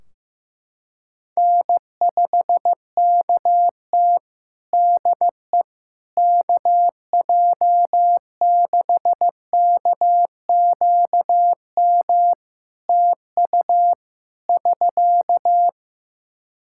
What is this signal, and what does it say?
Morse code, heard as a keyed tone, about 15 wpm, 700 Hz. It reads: N5KT DE KJ6KQM TU <SK>
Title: Practice QSO - Exchange 4 - Ending - 15wpm